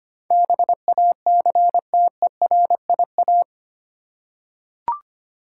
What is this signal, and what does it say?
Morse code, heard as a keyed tone, about 25 wpm, 700 Hz. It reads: BACTERIA
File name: Eight Letter Words - 25wpm